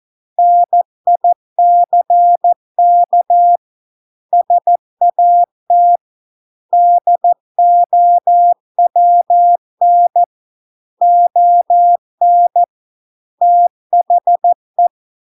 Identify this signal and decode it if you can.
Morse code, heard as a keyed tone, about 14 wpm, 700 Hz. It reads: NICK SAT DOWN ON THE